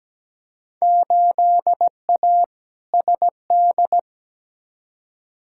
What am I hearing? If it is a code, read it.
Morse code, heard as a keyed tone, about 17 wpm, 700 Hz. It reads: 8A SD
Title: Field Day Exchanges Encore - 17wpm